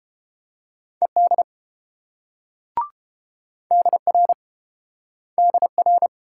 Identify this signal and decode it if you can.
Morse code, heard as a keyed tone, about 28 wpm, 700 Hz. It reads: ED DR DR